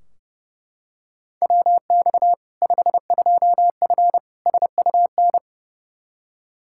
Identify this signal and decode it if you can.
Morse code, heard as a keyed tone, about 30 wpm, 700 Hz. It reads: WX 52F SUN
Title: Practice QSO - Exchange 2 - Info - 30wpm